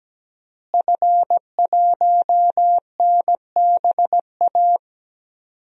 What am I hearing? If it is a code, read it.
Morse code, heard as a keyed tone, about 17 wpm, 700 Hz. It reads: F1NBA
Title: DXCC-100 Call Signs - 5 Characters - Repeated 2x - 17wpm